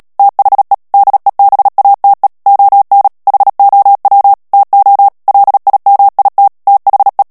Morse code, heard as a keyed tone, about 37 wpm, 800 Hz. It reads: THE DEBATE ON HOW TO LIMIT THE
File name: morse mp3 / ANGLAIS texte clair 2 vitesse 35 mots minute